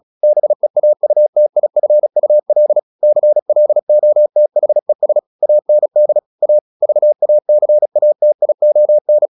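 Morse code, heard as a keyed tone, about 36 wpm, 600 Hz. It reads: BEAUTIFUL CLOTHES AND A VACATION
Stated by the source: Morse practice at 36 wpm, text of Ken Follett's novel Never, from CW med Gnister.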